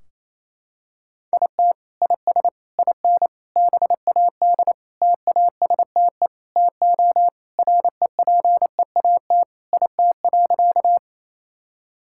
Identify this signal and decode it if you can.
Morse code, heard as a keyed tone, about 28 wpm, 700 Hz. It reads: IT IS IN BAD TASTE TO REPEAT IT.
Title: Sentences from Top 1600 Words 28wpm